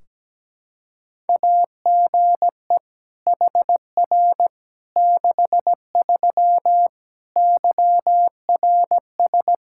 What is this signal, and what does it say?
Morse code, heard as a keyed tone, about 17 wpm, 700 Hz. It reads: AGE HR 63 YRS
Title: Practice QSO - Exchange 3 - More Info - 17wpm